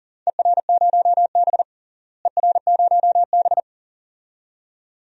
Morse code, heard as a keyed tone, about 39 wpm, 700 Hz. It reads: ER0B ER0B
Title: DXCC-100 Call Signs - 4 Characters - Repeated 2x - 40wpm